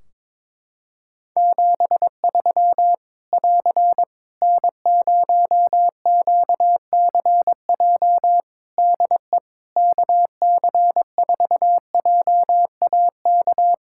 Morse code, heard as a keyed tone, about 22 wpm, 700 Hz. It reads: 73 <AR> N0QCJ DE KC4JAK
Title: K1USN SST QSO Practice 22wpm